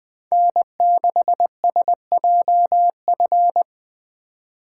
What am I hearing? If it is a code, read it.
Morse code, heard as a keyed tone, about 20 wpm, 700 Hz. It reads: N6SJF
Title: Speed Racing - USA 1x3 Call Signs 17wpm